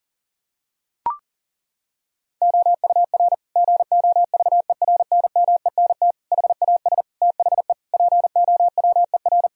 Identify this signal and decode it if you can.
Morse code, heard as a keyed tone, about 40 wpm, 700 Hz. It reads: OUR GOVERNMENT HAS THE POWER
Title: Sentences from Top 1000 Words - Review Part 3 - 40wpm